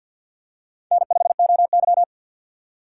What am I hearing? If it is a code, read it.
Morse code, heard as a keyed tone, about 53 wpm, 700 Hz. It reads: N5OY
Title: USA Call Signs - 4 Characters - 3x Character Spacing - 50wpm